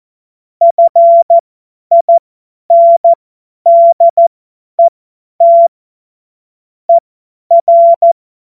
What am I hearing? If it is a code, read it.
Morse code, heard as a keyed tone, about 14 wpm, spaced as for 10 wpm, 680 Hz. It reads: FINDET ER
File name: Morsekurs / Witze / Witz 01. August 2023 50 F70